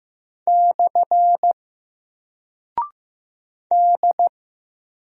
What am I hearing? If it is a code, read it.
Morse code, heard as a keyed tone, about 15 wpm, 700 Hz. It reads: / D
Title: Single Character (Letter:Number:Punctuation:Prosign) 15wpm